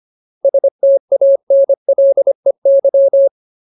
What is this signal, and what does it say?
Morse code, heard as a keyed tone, about 25 wpm, 550 Hz. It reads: STANLEY
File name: Ultimate Word List / 25WPM Ultimate